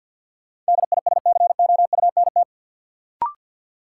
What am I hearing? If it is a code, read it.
Morse code, heard as a keyed tone, about 52 wpm, 700 Hz. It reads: DISCOUNT
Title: Sentences from Top 1200 Words - New Word Drill - 50wpm